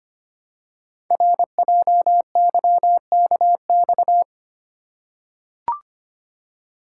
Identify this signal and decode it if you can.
Morse code, heard as a keyed tone, about 25 wpm, 700 Hz. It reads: RJYKX
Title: Code Groups - 5 Letters - 25wpm